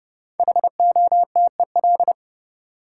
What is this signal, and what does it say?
Morse code, heard as a keyed tone, about 32 wpm, 700 Hz. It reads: HOTEL
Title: Sentences from Top 1600 Words - New Word Drill - 30wpm